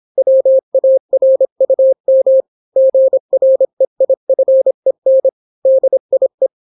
Morse code, heard as a keyed tone, about 26 wpm, 530 Hz. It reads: WARUM GREIFEN DIE